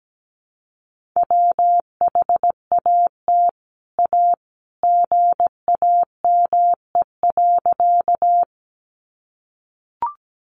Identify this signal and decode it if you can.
Morse code, heard as a keyed tone, about 17 wpm, 700 Hz. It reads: WHAT A GAME.
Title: Sentences from Top 500 Words 17wpm